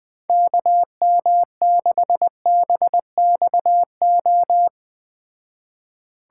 Morse code, heard as a keyed tone, about 19 wpm, 700 Hz. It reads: KM6BXO
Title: Speed Racing - USA 2x3 Call Signs 17wpm